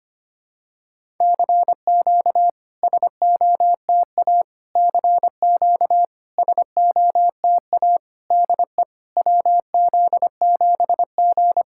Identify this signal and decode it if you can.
Morse code, heard as a keyed tone, about 25 wpm, 700 Hz. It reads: CQ SOTA CQ SOTA DE WZ7G
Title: SOTA QSO Practice 25wpm